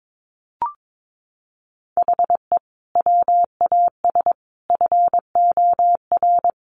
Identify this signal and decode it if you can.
Morse code, heard as a keyed tone, about 22 wpm, 700 Hz. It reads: HE WAS FOR
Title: Sentences from Top 1400 Words 22wpm